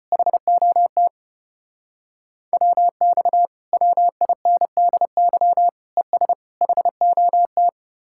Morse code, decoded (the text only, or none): HOT WX WINDY ES HOT